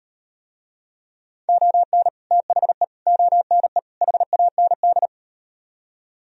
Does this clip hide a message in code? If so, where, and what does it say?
Yes: Morse code from 1.5 to 5.1 s, heard as a keyed tone, about 38 wpm, 700 Hz. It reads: ON THE ONE HAND